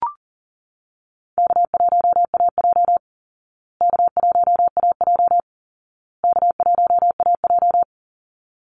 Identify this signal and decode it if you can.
Morse code, heard as a keyed tone, about 40 wpm, 700 Hz. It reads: K1AJ K1AJ K1AJ
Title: Call Signs Repeated 2x - 40wpm